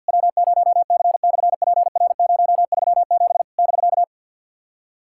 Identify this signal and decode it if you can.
Morse code, heard as a keyed tone, about 48 wpm, 700 Hz. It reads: W0C/PR037 <BK>
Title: SOTA QSO Practice 50wpm